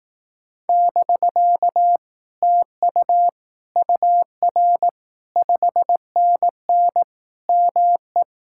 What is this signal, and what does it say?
Morse code, heard as a keyed tone, about 18 wpm, 700 Hz. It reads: <BK> TU UR 5NN ME
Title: Speed-Racing Common QSO Phrases Encore - 15wpm